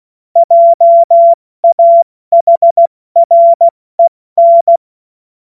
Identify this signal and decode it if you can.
Morse code, heard as a keyed tone, about 16 wpm, 680 Hz. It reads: JAHREN
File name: Morsekurs / Witze / Witz 19. April 2022 70 F80